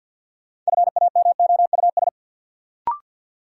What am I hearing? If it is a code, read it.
Morse code, heard as a keyed tone, about 49 wpm, 700 Hz. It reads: FAMOUS
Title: Sentences from Top 1500 Words - New Word Drill Alt - 50wpm